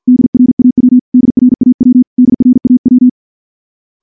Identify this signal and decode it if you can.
Morse code, heard as a keyed tone, about 50 wpm, 250 Hz. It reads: DRAW DRAW DRAW
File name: morse / morse-training-50WPM